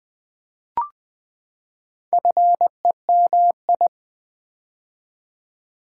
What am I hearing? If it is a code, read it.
Morse code, heard as a keyed tone, about 20 wpm, 700 Hz. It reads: FEMI
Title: Build A Word - 4 Letters At A Time - 8-Letter Words - 20wpm